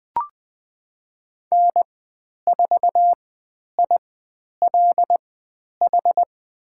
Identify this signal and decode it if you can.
Morse code, heard as a keyed tone, about 20 wpm, spaced as for 10 wpm, 700 Hz. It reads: N4ILH
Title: 095 Lesson 16g - Letter c - Call Signs 20wpm